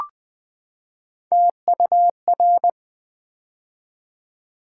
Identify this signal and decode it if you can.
Morse code, heard as a keyed tone, about 20 wpm, 700 Hz. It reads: TUR
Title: Build A Word - 3 Letters At A Time - 8-Letter Words - 20wpm